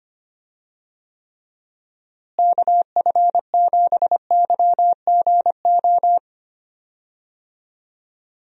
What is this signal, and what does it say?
Morse code, heard as a keyed tone, about 25 wpm, 700 Hz. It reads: KF7YGO
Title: USA Call Signs - 6 Characters - Repeated 2x - 25wpm